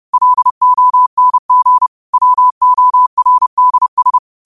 Morse code, heard as a keyed tone, about 30 wpm, 1000 Hz. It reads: LONG WORDS
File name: qotdfiles / 30wpm